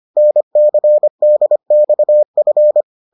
Morse code, heard as a keyed tone, about 25 wpm, 600 Hz. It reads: NCDXF